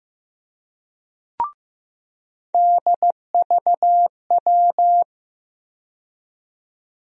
Morse code, heard as a keyed tone, about 15 wpm, 700 Hz. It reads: DVW
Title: USA Call Sign Postfixes - 3 Letters 15wpm